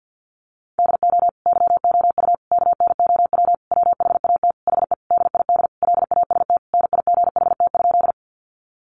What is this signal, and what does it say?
Morse code, heard as a keyed tone, about 50 wpm, 700 Hz. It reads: DO YOU KNOW WHAT HE DID LAST NIGHT?
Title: Sentences from Top 300 Words 50wpm